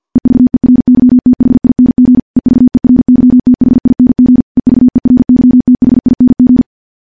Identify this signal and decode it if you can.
Morse code, heard as a keyed tone, about 50 wpm, 250 Hz. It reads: EVERYTHING EVERYTHING EVERYTHING
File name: morse / morse-training-50WPM